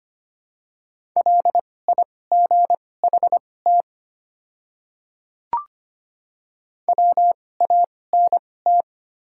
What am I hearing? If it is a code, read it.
Morse code, heard as a keyed tone, about 25 wpm, spaced as for 18 wpm, 700 Hz. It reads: LIGHT WANT
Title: 2x Character Spacing - Top 200 Words - 25wpm